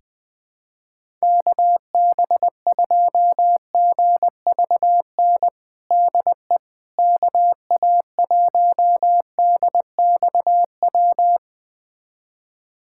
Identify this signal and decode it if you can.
Morse code, heard as a keyed tone, about 20 wpm, 700 Hz. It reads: KB2GVN DE KA1DXW
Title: USA Callsign DE Callsign - 6 Characters - 20wpm